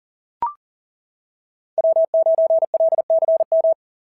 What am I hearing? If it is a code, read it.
Morse code, heard as a keyed tone, about 40 wpm, 690 Hz. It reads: W9LCM
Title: SOTA QSO Practice 40wpm